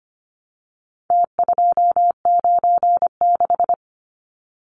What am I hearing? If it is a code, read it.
Morse code, heard as a keyed tone, about 25 wpm, 700 Hz. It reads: T296
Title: Numbers - Contest Style - 4-Digit 25wpm